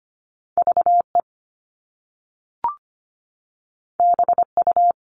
Morse code, heard as a keyed tone, about 25 wpm, 700 Hz. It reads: V E B U
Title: All Cut Numbers - 2-Digit 25wpm